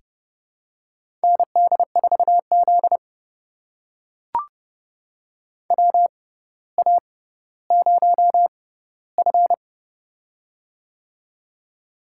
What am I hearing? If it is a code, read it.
Morse code, heard as a keyed tone, about 30 wpm, spaced as for 15 wpm, 700 Hz. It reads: ND4Z WA0F
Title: USA Call Signs - 4 Characters - 6x Character Spacing - 30wpm